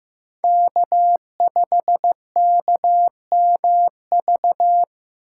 Morse code, heard as a keyed tone, about 15 wpm, 700 Hz. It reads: K5KMV
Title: USA Call Signs - 5 Characters - 4x Character Spacing - 15wpm